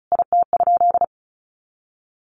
Morse code, heard as a keyed tone, about 34 wpm, 700 Hz. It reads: IT?
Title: Sentences from Top 1100 Words 35wpm